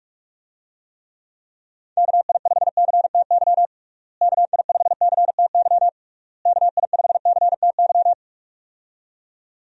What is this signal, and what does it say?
Morse code, heard as a keyed tone, about 46 wpm, 700 Hz. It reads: KI5CTY KI5CTY KI5CTY
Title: USA Call Signs - 6 Characters - Repeated 3x - 45wpm